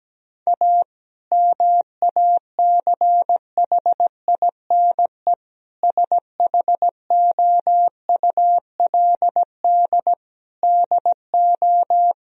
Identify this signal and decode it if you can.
Morse code, heard as a keyed tone, about 17 wpm, 700 Hz. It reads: A MACHINE SHOULD DO
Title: Sentences from Top 1000 Words - Review Part 2 - 17wpm